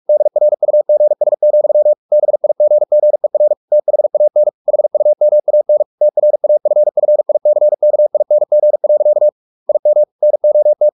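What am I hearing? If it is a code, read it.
Morse code, heard as a keyed tone, about 44 wpm, 600 Hz. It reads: DRUGS, BIGGER THAN HUMAN TRAFFICKING. IM NOT
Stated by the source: Morse practice at 45 wpm, the novel Never by Ken Follett in English, from CW med Gnister.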